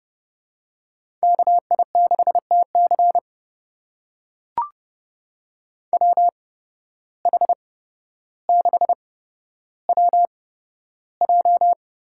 Morse code, heard as a keyed tone, about 30 wpm, spaced as for 13 wpm, 700 Hz. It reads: KI6TC WH6WJ
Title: USA Call Signs - 5 Characters - 8x Character Spacing - 30wpm